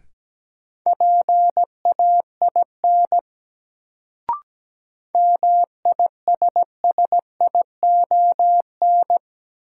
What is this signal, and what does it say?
Morse code, heard as a keyed tone, about 17 wpm, 700 Hz. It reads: PAIN MISSION
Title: Sentences from Top 1600 Words - New Word Drill - 17wpm